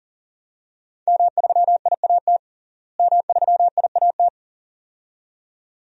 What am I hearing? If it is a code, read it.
Morse code, heard as a keyed tone, about 40 wpm, 700 Hz. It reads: M3IAT M3IAT
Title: DXCC-100 Call Signs - 5 Characters - Repeated 2x - 40wpm